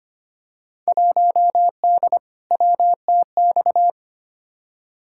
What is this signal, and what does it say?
Morse code, heard as a keyed tone, about 25 wpm, 700 Hz. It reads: 1D WTX
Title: Field Day Exchanges - 25wpm